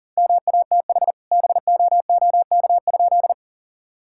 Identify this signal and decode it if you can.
Morse code, heard as a keyed tone, about 40 wpm, 700 Hz. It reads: MATH BOOK?
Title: Sentences from Top 1000 Words - Review Part 6 - 40wpm